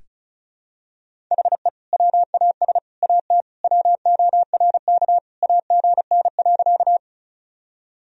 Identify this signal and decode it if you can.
Morse code, heard as a keyed tone, about 34 wpm, 700 Hz. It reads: HE WAS AT WORK AGN.
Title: Sentences from Top 200 Words and Abbreviations 35wpm